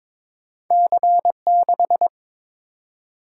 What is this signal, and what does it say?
Morse code, heard as a keyed tone, about 22 wpm, 700 Hz. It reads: C6
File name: practice / 22wpm / Two Letter-Number Combinations - 22wpm